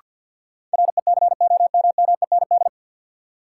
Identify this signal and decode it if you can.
Morse code, heard as a keyed tone, about 48 wpm, 700 Hz. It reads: RECOMMEND
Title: Sentences from Top 1200 Words - New Word Drill Alt - 50wpm